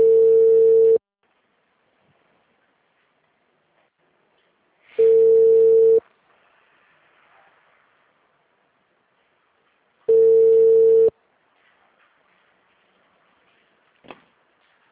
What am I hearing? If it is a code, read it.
Ringback tone (China, 450 Hz).